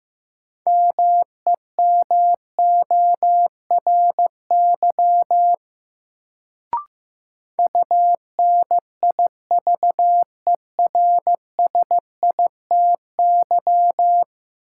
Morse code, heard as a keyed tone, about 15 wpm, 700 Hz. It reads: MEMORY UNIVERSITY
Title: Sentences from Top 1300 Words - New Word Drill Alt - 15wpm